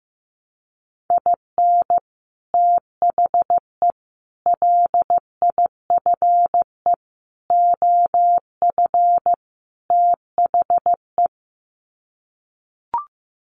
Morse code, heard as a keyed tone, about 15 wpm, 700 Hz. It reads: IN THE LIFE OF THE
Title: Sets of 5 Words Encore2 15wpm